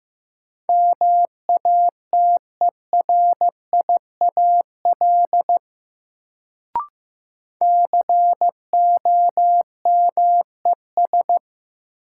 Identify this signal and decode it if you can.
Morse code, heard as a keyed tone, about 15 wpm, 700 Hz. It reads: MATERIAL COMES
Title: Sentences from Top 1000 Words - Review Part 5 - New Word Drill - 15wpm